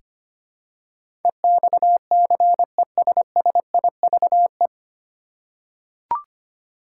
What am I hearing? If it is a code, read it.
Morse code, heard as a keyed tone, about 25 wpm, 700 Hz. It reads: EXCESSIVE
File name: Nine Letter Words - 25wpm